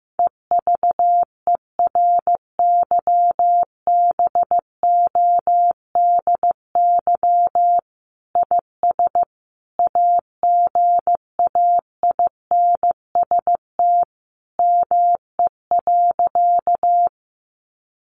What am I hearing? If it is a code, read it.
Morse code, heard as a keyed tone, about 15 wpm, 700 Hz. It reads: EVERYBODY IS AGAINST ME.
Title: Sentences from Top 1400 Words 15wpm